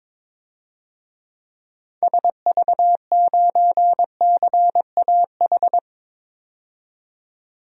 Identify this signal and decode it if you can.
Morse code, heard as a keyed tone, about 22 wpm, 700 Hz. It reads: SV9CAH
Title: DXCC-100 Call Signs - 6 Characters - 22wpm